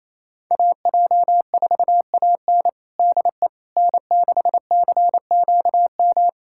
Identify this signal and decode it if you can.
Morse code, heard as a keyed tone, about 28 wpm, 700 Hz. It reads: AJ4AN DE N6CQM
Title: USA Callsign DE Callsign - 5 Characters - 28wpm